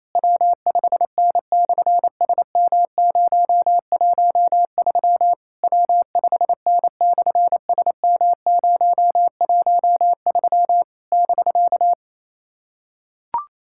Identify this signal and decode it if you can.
Morse code, heard as a keyed tone, about 28 wpm, 700 Hz. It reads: W5N/SM013 W5N/SM013 <BK>
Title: SOTA QSO Practice 28wpm